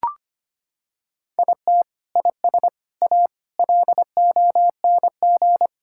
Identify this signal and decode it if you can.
Morse code, heard as a keyed tone, about 25 wpm, 700 Hz. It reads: IT IS A LONG
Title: Sentences from Top 1000 Words - Review Part 5 - 25wpm